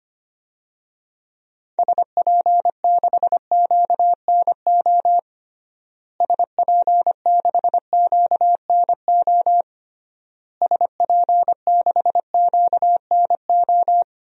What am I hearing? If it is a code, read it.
Morse code, heard as a keyed tone, about 25 wpm, 700 Hz. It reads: SP6QNO SP6QNO SP6QNO
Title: DXCC-100 Call Signs - 6 Characters - Repeated 3x - 25wpm